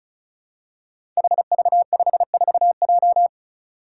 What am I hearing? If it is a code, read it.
Morse code, heard as a keyed tone, about 36 wpm, 700 Hz. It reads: HV54J
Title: Five Challenging Letter-Number Combinations 35wpm